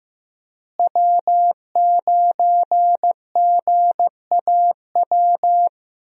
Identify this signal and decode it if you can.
Morse code, heard as a keyed tone, about 15 wpm, 700 Hz. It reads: W9GAW
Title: USA Call Signs - 5 Characters - Repeated 2x - 15wpm